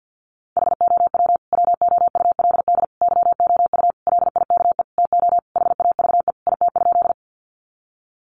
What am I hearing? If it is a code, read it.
Morse code, heard as a keyed tone, about 50 wpm, 700 Hz. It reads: HOW WOULD YOU LIKE TO HAVE IT?
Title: Sentences from Top 100 Words 50wpm